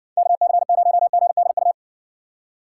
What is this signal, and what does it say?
Morse code, heard as a keyed tone, about 56 wpm, 700 Hz. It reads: KC0GDU